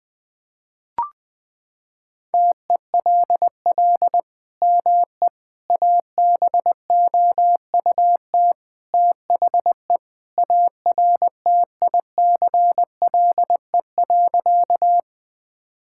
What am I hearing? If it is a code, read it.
Morse code, heard as a keyed tone, about 20 wpm, 700 Hz. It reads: TELL ME ABOUT THE ARTICLE.
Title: Sentences from Top 1000 Words - Review Part 5 - 20wpm